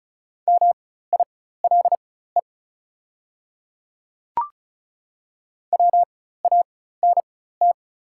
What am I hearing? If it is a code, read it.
Morse code, heard as a keyed tone, about 35 wpm, spaced as for 16 wpm, 690 Hz. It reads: MILE WANT
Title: 4x Character Spacing - Top 500 Words - 35wpm